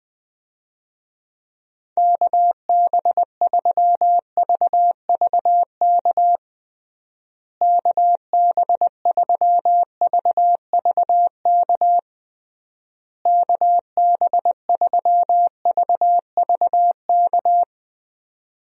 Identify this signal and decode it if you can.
Morse code, heard as a keyed tone, about 20 wpm, 700 Hz. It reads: KB3VVK KB3VVK KB3VVK
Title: USA Call Signs - 6 Characters - Repeated 3x - 20wpm